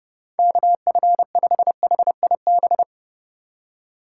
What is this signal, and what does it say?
Morse code, heard as a keyed tone, about 30 wpm, 700 Hz. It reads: KF5HIB
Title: USA Call Signs - 6 Characters - 2x Character Spacing - 30wpm